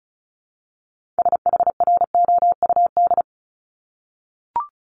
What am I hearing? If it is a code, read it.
Morse code, heard as a keyed tone, about 36 wpm, 700 Hz. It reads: SHROUD
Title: Multi-letter Phonemes Repeated 3x 35wpm